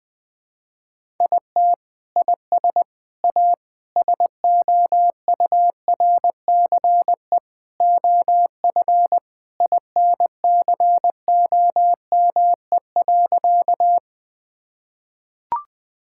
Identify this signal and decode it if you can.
Morse code, heard as a keyed tone, about 20 wpm, 700 Hz. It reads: IT IS A SOURCE OF INCOME.
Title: Sentences from Top 1000 Words - Review Part 5 - 20wpm